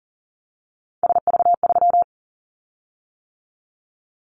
Morse code, heard as a keyed tone, about 42 wpm, 700 Hz. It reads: SV3
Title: Callsign Prefixes for Top 50 DXCC - 40wpm